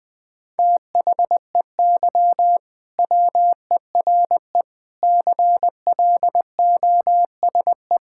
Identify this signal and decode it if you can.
Morse code, heard as a keyed tone, about 20 wpm, 700 Hz. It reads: THEY WERE CLOSE
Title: Sentences from Top 1000 Words - Review Part 6 - 20wpm